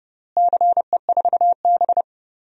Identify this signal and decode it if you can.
Morse code, heard as a keyed tone, about 30 wpm, 700 Hz. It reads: CE4B
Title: DXCC-100 Call Signs - 4 Characters - Repeated 2x - 30wpm